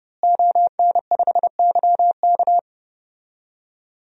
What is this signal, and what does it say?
Morse code, heard as a keyed tone, about 30 wpm, 700 Hz. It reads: ON5YK
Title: DXCC-100 Call Signs - 5 Characters - Repeated 2x - 30wpm